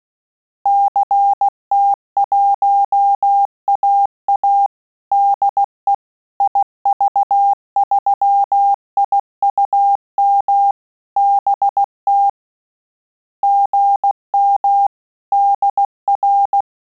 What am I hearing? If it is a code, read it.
Morse code, heard as a keyed tone, about 16 wpm, 800 Hz. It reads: CT1AA DE IV3IUM BT GM DR
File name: QSO CW 75 CPM